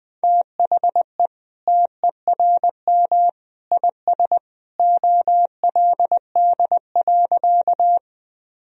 Morse code, heard as a keyed tone, about 20 wpm, 700 Hz. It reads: THE TERM IS OLD.
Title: Sentences from Top 1000 Words - Review Part 4 - 20wpm